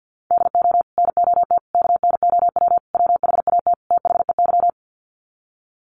Morse code, heard as a keyed tone, about 50 wpm, 700 Hz. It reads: DO NOT KNOW WHAT THEY
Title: Sets of 5 Words 50wpm